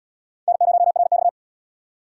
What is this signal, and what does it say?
Morse code, heard as a keyed tone, about 72 wpm, 700 Hz. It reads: N0RX